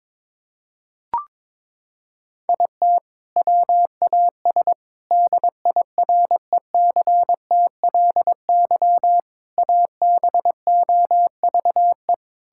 Morse code, heard as a keyed tone, about 21 wpm, 700 Hz. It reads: IT WAS DIRECTLY ABOVE
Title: Sentences from Top 1000 Words - Review Part 5 - 22wpm